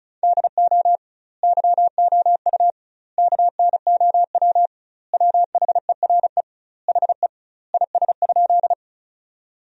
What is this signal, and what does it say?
Morse code, heard as a keyed tone, about 36 wpm, 700 Hz. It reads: DO YOU KNOW WHERE HE IS?
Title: 2x Word-Spacing - Sentences from Top 300 Words - 35wpm